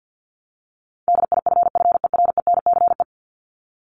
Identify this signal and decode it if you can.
Morse code, heard as a keyed tone, about 50 wpm, 700 Hz. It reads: DIFFERENCE
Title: ES:S Suffix - Top 100 Words with Base Words 50wpm